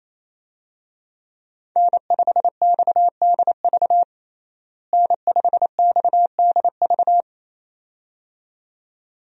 Morse code, heard as a keyed tone, about 27 wpm, 700 Hz. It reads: N5XDV N5XDV
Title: USA Call Signs - 5 Characters - Repeated 2x - 28wpm